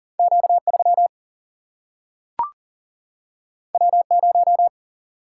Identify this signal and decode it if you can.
Morse code, heard as a keyed tone, about 40 wpm, 700 Hz. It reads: Q 3 W 0
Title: Single-Letter Single-Number 40wpm